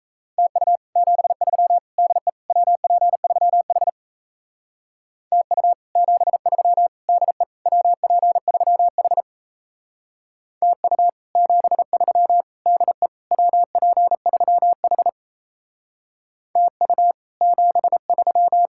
Morse code, heard as a keyed tone, about 35 wpm, 700 Hz. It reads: TU 73 DE WP3H TU 73 DE WP3H TU 73 DE WP3H TU 73